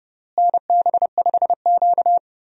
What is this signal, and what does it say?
Morse code, heard as a keyed tone, about 32 wpm, 700 Hz. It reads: NB5Q
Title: USA Call Signs - 4 Characters - Repeated 3x - 30wpm